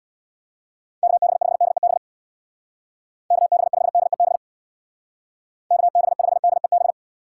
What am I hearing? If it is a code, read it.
Morse code, heard as a keyed tone, about 61 wpm, 700 Hz. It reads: KB5DEB KB5DEB KB5DEB
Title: Speed Racing - USA Call Signs 50wpm